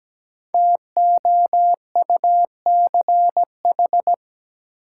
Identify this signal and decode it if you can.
Morse code, heard as a keyed tone, about 17 wpm, 700 Hz. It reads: TOUCH
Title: Sentences from Top 1200 Words - New Word Drill - 17wpm